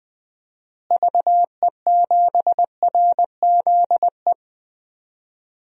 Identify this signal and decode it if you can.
Morse code, heard as a keyed tone, about 20 wpm, 700 Hz. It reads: VE7RZE
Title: DXCC-100 Call Signs - 6 Characters - Repeated 3x - 20wpm